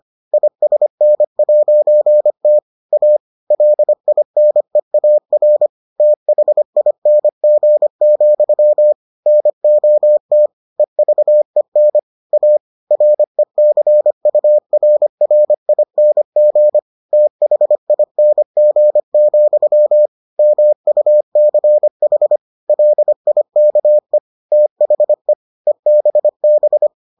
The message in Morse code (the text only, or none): ISN'T A LINEAR THING, NOT EVEN A RECURRING THING, MUCH LIKE THE EBB